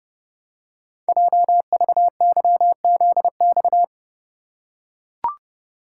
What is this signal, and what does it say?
Morse code, heard as a keyed tone, about 29 wpm, 700 Hz. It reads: JVYZX
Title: Code Groups - 5 Letters - 30wpm